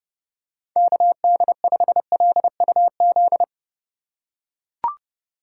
Morse code, heard as a keyed tone, about 27 wpm, 700 Hz. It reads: KD5LUZ